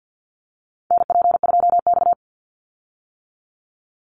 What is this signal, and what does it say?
Morse code, heard as a keyed tone, about 50 wpm, 700 Hz. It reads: NP2X
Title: Call Signs - 50wpm